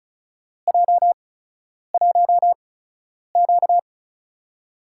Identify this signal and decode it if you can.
Morse code, heard as a keyed tone, about 34 wpm, spaced as for 10 wpm, 700 Hz. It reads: J1Q
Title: USA Call Signs - 4 Characters - 8x Character Spacing - 35wpm